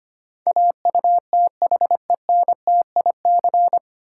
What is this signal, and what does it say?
Morse code, heard as a keyed tone, about 25 wpm, 700 Hz. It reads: AUTHENTIC
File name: Nine Letter Words - 25wpm